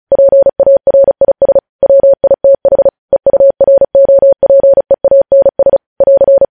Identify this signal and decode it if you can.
Morse code, heard as a keyed tone, about 36 wpm, 550 Hz. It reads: PARIS WITH EUROPEANS <AR>
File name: podcast / News headlines-1x-35x35